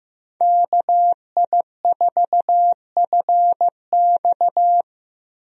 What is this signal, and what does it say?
Morse code, heard as a keyed tone, about 15 wpm, 700 Hz. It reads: KI4FX
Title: USA Call Signs - 5 Characters - Repeated 3x - 15wpm